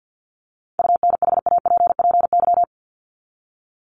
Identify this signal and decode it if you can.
Morse code, heard as a keyed tone, about 48 wpm, 700 Hz. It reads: UNHAPPY
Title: Top 1501-1750 Words 50wpm